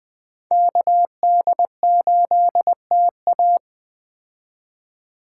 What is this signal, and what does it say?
Morse code, heard as a keyed tone, about 21 wpm, 700 Hz. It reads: KD8TA